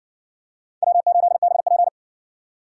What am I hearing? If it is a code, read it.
Morse code, heard as a keyed tone, about 56 wpm, 700 Hz. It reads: W8BP